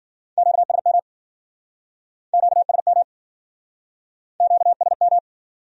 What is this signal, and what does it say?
Morse code, heard as a keyed tone, about 52 wpm, 700 Hz. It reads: QSM QSM QSM